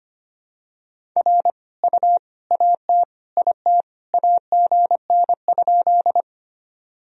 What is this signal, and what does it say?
Morse code, heard as a keyed tone, about 25 wpm, 700 Hz. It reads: R U AT IT AGN?
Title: Sentences from Top 200 Words and Abbreviations 25wpm